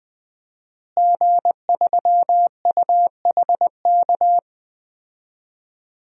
Morse code, heard as a keyed tone, about 21 wpm, 700 Hz. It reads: G3UHK
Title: DXCC-100 Call Signs - 5 Characters - Repeated 3x - 20wpm